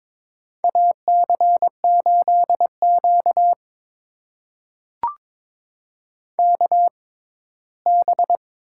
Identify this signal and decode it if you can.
Morse code, heard as a keyed tone, about 22 wpm, spaced as for 15 wpm, 700 Hz. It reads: AC8Q KB
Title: USA Call Signs - 6x Character Spacing - 22wpm